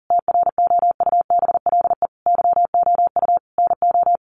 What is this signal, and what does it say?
Morse code, heard as a keyed tone, about 40 wpm, 700 Hz. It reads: TROUBLE YOU NO
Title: Sentences from Top 1200 Words 40wpm